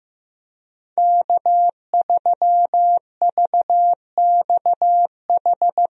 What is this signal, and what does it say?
Morse code, heard as a keyed tone, about 15 wpm, 700 Hz. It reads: K3VXH
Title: USA Call Signs - 5 Characters - Repeated 2x - 15wpm